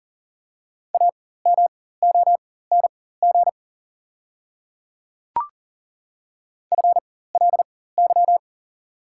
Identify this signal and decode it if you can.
Morse code, heard as a keyed tone, about 40 wpm, spaced as for 19 wpm, 700 Hz. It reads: AMONG FLY
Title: 4x Character Spacing - Top 500 Words - 40wpm